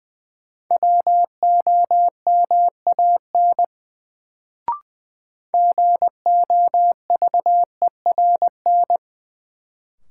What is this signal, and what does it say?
Morse code, heard as a keyed tone, about 20 wpm, 700 Hz. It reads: WOMAN GOVERN
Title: Sentences from Top 1000 Words - Review Part 2 - New Word Drill - 20wpm